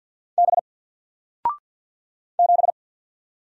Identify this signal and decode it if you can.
Morse code, heard as a keyed tone, about 48 wpm, 700 Hz. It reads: B 7
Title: Single Letter-Number 50wpm